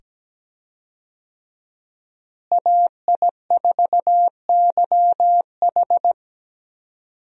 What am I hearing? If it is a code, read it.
Morse code, heard as a keyed tone, about 17 wpm, 700 Hz. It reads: AI4YH
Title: USA Call Signs - 5 Characters - Repeated 2x - 17wpm